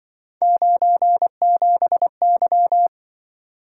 Morse code, heard as a keyed tone, about 25 wpm, 700 Hz. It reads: 97Y